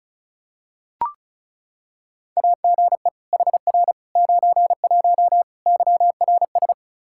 Morse code, heard as a keyed tone, about 36 wpm, 700 Hz. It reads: AGE HR 91 YRS
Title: K1USN SST QSO Practice 35wpm